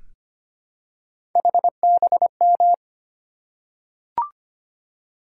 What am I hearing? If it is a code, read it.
Morse code, heard as a keyed tone, about 25 wpm, 700 Hz. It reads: HBM
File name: Three Letter-Number Combinations - 25wpm